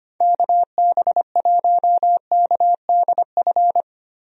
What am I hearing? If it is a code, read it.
Morse code, heard as a keyed tone, about 25 wpm, 700 Hz. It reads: KB1KDF
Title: USA Call Signs - 6 Characters - Repeated 3x - 25wpm